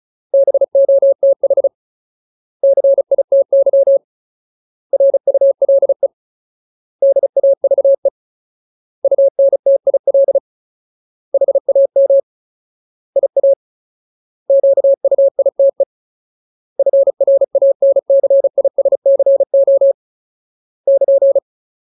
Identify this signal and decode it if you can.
Morse code, heard as a keyed tone, about 36 wpm, 550 Hz. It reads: BOTH CITY RULE DAVE UNTIL HAM IA QUITE FRANCISCO <KN>
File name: Ultimate Word List / 35WPM Ultimate